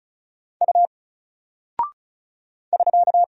Morse code, heard as a keyed tone, about 36 wpm, 700 Hz. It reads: U <SK>
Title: Single Character (Letter:Number:Punctuation:Prosign) Alt 35wpm